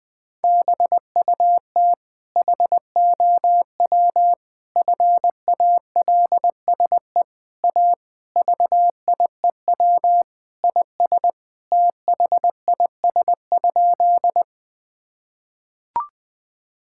Morse code, heard as a keyed tone, about 20 wpm, 700 Hz. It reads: BUT HOW FALSE A VIEW IS THIS?
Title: Sentences from Top 1600 Words 20wpm